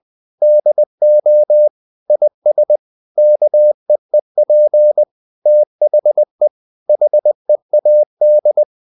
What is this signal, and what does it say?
Morse code, heard as a keyed tone, about 20 wpm, 600 Hz. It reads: DO IS KEEP THE HEAD
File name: the old man and the sea part 2 20wpm